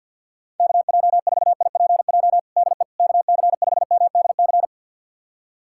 Morse code, heard as a keyed tone, about 51 wpm, 700 Hz. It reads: KJ4IPJ DE KC5MDC
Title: USA Callsign DE Callsign - 6 Characters - 50wpm